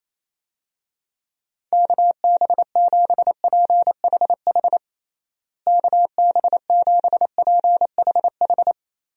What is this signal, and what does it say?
Morse code, heard as a keyed tone, about 28 wpm, 700 Hz. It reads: KB7PHH KB7PHH
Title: USA Call Signs - 6 Characters - Repeated 3x - 28wpm